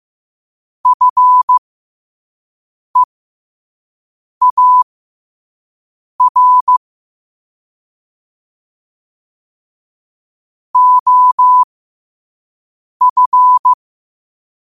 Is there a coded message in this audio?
Yes: Morse code from 0.9 to 13.7 s, heard as a keyed tone, about 15 wpm, spaced as for 5 wpm, 1000 Hz. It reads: FEAR OF